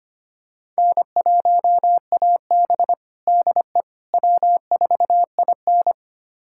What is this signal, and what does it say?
Morse code, heard as a keyed tone, about 25 wpm, 700 Hz. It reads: N1AB DE W4IN
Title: USA Callsign DE Callsign - 4 Characters - 25wpm